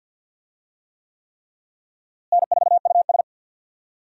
Morse code, heard as a keyed tone, about 53 wpm, 700 Hz. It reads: N4US
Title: USA Call Signs - 4 Characters - Repeated 3x - 50wpm